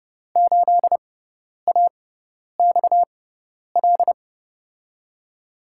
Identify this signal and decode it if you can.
Morse code, heard as a keyed tone, about 30 wpm, spaced as for 10 wpm, 700 Hz. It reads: 8AXL
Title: USA Call Signs - 6x Character Spacing - 30wpm